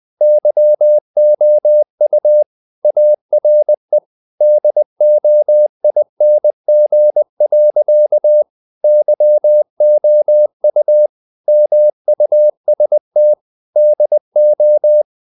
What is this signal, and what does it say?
Morse code, heard as a keyed tone, about 20 wpm, 600 Hz. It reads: YOU ARE DOING. YOU MUST DO
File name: the old man and the sea / the old man and the sea part 1 20wpm